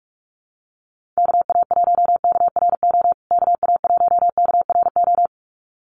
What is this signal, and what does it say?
Morse code, heard as a keyed tone, about 44 wpm, 700 Hz. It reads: KA1KRO KA1KRO
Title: USA Call Signs Repeated 2x 45wpm